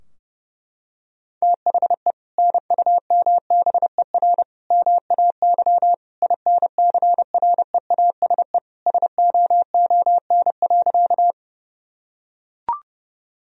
Morse code, heard as a keyed tone, about 30 wpm, 700 Hz. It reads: THE NUMBER MAY INCREASE SOON.
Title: Sentences from Top 1000 Words - Review Part 4 - 30wpm